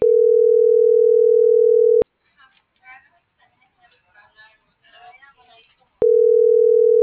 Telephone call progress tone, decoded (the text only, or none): ringback tone